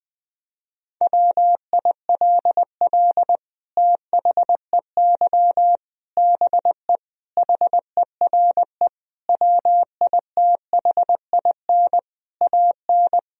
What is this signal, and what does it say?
Morse code, heard as a keyed tone, about 20 wpm, 700 Hz. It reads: WILL THEY BE HERE WITHIN AN
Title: Sentences from Top 1000 Words - Review Part 4 - 20wpm